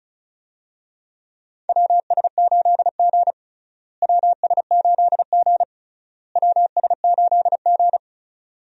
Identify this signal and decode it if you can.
Morse code, heard as a keyed tone, about 34 wpm, 700 Hz. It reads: WS8G WS8G WS8G
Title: USA Call Signs - 4 Characters - Repeated 3x - 35wpm